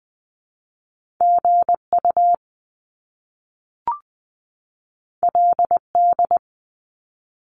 Morse code, heard as a keyed tone, about 20 wpm, 700 Hz. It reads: G U L D
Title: Top Two Letter Combinations in Words 20wpm